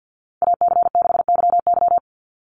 Morse code, heard as a keyed tone, about 50 wpm, 700 Hz. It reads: AC6YY
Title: Call Signs Repeated 2x - 50wpm